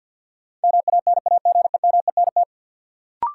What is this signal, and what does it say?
Morse code, heard as a keyed tone, about 53 wpm, 700 Hz. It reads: MANAGEMENT
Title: Sentences from Top 1000 Words - Review Part 4 - New Word Drill - 50wpm